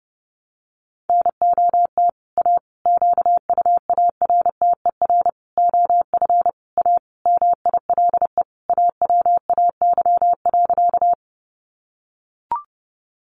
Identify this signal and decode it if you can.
Morse code, heard as a keyed tone, about 30 wpm, 700 Hz. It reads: NOT A QUARTER OF A MILE AWAY.
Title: Sentences from Top 1300 Words 30wpm